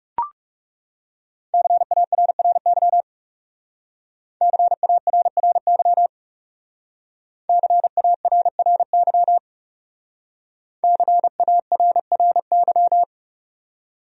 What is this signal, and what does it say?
Morse code, heard as a keyed tone, about 38 wpm, 700 Hz. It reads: CARRY CARRY CARRY CARRY